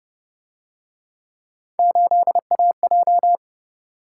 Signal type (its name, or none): Morse code